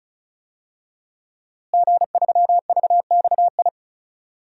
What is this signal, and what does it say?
Morse code, heard as a keyed tone, about 36 wpm, 700 Hz. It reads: G3VXI
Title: DXCC-100 Call Signs - 5 Characters - Repeated 2x - 35wpm